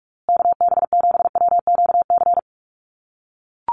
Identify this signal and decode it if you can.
Morse code, heard as a keyed tone, about 44 wpm, 700 Hz. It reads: KB7WQC
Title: USA Call Signs Repeated 2x 45wpm